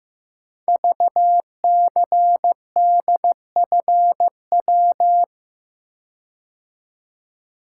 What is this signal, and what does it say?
Morse code, heard as a keyed tone, about 15 wpm, 700 Hz. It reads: VCDFW
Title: Code Groups - 5 Letters - 15wpm